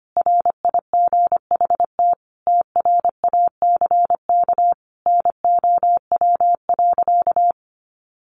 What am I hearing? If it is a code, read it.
Morse code, heard as a keyed tone, about 25 wpm, 690 Hz. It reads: RIGHT TRACK NOW.
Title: Sentences from Top 1100 Words 25wpm